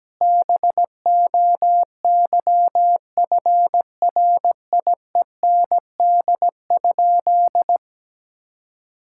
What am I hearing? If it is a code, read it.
Morse code, heard as a keyed tone, about 17 wpm, 700 Hz. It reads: BOYFRIEND?
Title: Sentences from Top 1000 Words - Review Part 5 - 17wpm